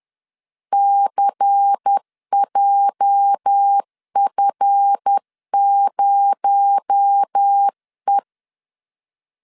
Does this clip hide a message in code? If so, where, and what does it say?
Yes: Morse code from 0.7 to 8.2 s, heard as a keyed tone, about 10 wpm, 800 Hz. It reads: CJF0E